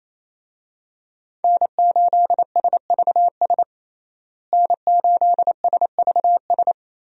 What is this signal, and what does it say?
Morse code, heard as a keyed tone, about 28 wpm, 700 Hz. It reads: N8SVS N8SVS
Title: USA Call Signs - 5 Characters - Repeated 2x - 28wpm